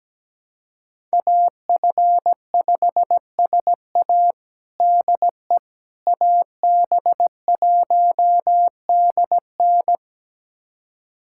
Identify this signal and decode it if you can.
Morse code, heard as a keyed tone, about 17 wpm, 700 Hz. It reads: AF5SA DE AB1DN
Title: USA Callsign DE Callsign - 5 Characters - 17wpm